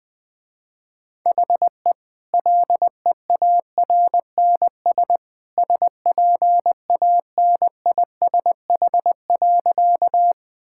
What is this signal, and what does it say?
Morse code, heard as a keyed tone, about 21 wpm, 700 Hz. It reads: HE LEARNS SPANISH.
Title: Easy Sentences with Hints - 20wpm